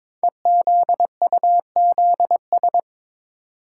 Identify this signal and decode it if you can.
Morse code, heard as a keyed tone, about 22 wpm, 700 Hz. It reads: EZUZS
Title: Code Groups - 5 Letters - 22wpm